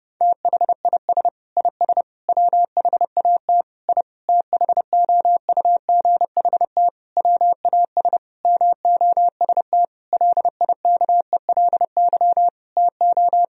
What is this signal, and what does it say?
Morse code, heard as a keyed tone, about 30 wpm, 700 Hz. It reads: THIS IS WHAT I THOUGHT WAS MOST LIKELY TO
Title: Sentences from Top 1000 Words - Review Part 4 - 30wpm